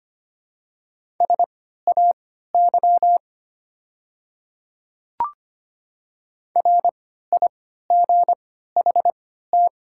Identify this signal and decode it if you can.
Morse code, heard as a keyed tone, about 25 wpm, spaced as for 14 wpm, 700 Hz. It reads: SAY RIGHT
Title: 3x Character Spacing - Top 200 Words - 25wpm